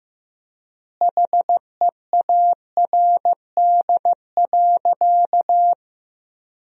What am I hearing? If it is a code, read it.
Morse code, heard as a keyed tone, about 15 wpm, 700 Hz. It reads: HEARD.
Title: 3x Word-Spacing - Sentences from Top 400 Words - 15wpm